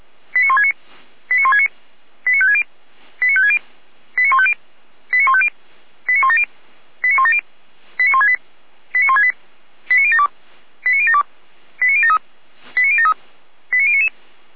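Five-tone selective call (ZVEI), heard as quick runs of five tones, tones of about 70 ms. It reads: 89169 89169 89560 89560 89150 89150 89170 89170 89168 89168 89072 89072 89073 89073 89099
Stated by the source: ZVEI five-tone selective call received by SDR, from the Feuerwehr Orlamünde fire brigade audio archive.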